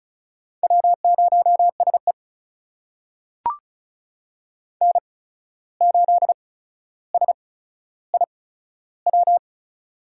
Morse code, heard as a keyed tone, about 36 wpm, spaced as for 14 wpm, 700 Hz. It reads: W0SE N8SIW
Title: USA Call Signs - 8x Character Spacing - 35wpm